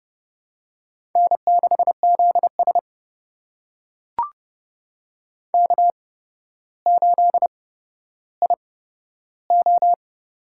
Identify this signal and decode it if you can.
Morse code, heard as a keyed tone, about 30 wpm, spaced as for 13 wpm, 700 Hz. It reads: N6ZS K8IO
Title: USA Call Signs - 4 Characters - 8x Character Spacing - 30wpm